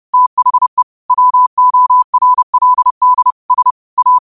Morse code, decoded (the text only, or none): THE WORLD S A